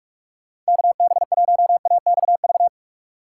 Morse code, heard as a keyed tone, about 55 wpm, 700 Hz. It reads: KB1AXV
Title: Speed Racing - USA 2x3 Call Signs 45wpm